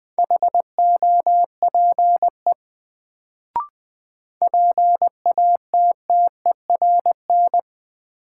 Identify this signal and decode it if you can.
Morse code, heard as a keyed tone, about 20 wpm, 700 Hz. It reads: HOPE PATTERN
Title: Sentences from Top 1000 Words - New Word Drill - 20wpm